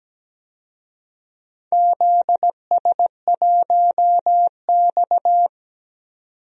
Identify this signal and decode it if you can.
Morse code, heard as a keyed tone, about 17 wpm, 700 Hz. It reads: ZS1X
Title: DXCC-100 Call Signs - 4 Characters - 17wpm